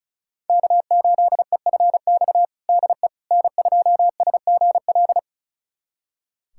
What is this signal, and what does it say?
Morse code, heard as a keyed tone, about 36 wpm, 700 Hz. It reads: K8EFX DE N2SGL
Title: USA Callsign DE Callsign - 5 Characters - 35wpm